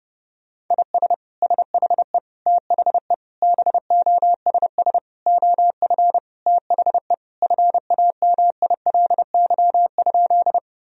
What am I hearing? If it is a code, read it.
Morse code, heard as a keyed tone, about 32 wpm, 700 Hz. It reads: IS SHE THE BOSS OF THE FAMILY?
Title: Sentences from Top 1000 Words - Review Part 4 - 30wpm